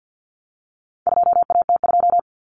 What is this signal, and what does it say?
Morse code, heard as a keyed tone, about 50 wpm, 700 Hz. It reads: 2AT2
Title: Common Cut Numbers - 4-Digit 50wpm